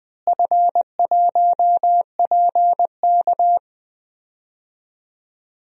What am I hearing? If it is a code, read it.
Morse code, heard as a keyed tone, about 20 wpm, 700 Hz. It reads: F1PK
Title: DXCC-100 Call Signs - 4 Characters - Repeated 3x - 20wpm